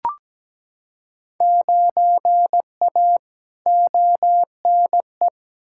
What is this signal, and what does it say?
Morse code, heard as a keyed tone, about 17 wpm, 700 Hz. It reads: 9A ONE
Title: Field Day Exchanges Encore - 17wpm